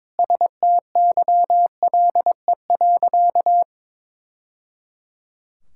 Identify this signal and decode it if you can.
Morse code, heard as a keyed tone, about 22 wpm, 700 Hz. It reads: STYLE.
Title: Sentences from Top 1000 Words - Review Part 5 - 22wpm